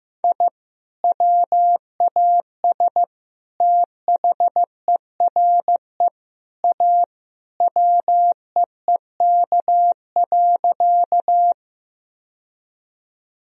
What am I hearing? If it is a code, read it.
Morse code, heard as a keyed tone, about 15 wpm, 700 Hz. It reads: I WAS THERE A WEEK.
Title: Sentences from Top 1000 Words - Review Part 2 - 15wpm